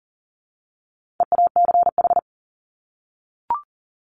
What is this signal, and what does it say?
Morse code, heard as a keyed tone, about 40 wpm, 700 Hz. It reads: EACH
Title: Top 200 Words 40wpm